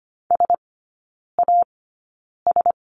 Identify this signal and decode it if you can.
Morse code, heard as a keyed tone, about 25 wpm, spaced as for 9 wpm, 700 Hz. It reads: SAS
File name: Lesson 17e - Letter u - Three Characters 25wpm